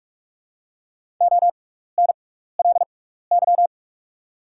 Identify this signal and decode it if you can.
Morse code, heard as a keyed tone, about 48 wpm, spaced as for 16 wpm, 700 Hz. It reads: ONLY
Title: 6x Character Spacing - Top 300 Words - 45wpm